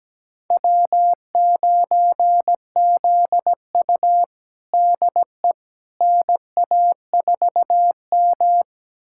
Morse code, heard as a keyed tone, about 17 wpm, 700 Hz. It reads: W9ZU DE NA4M
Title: USA Callsign DE Callsign - 4 Characters - 17wpm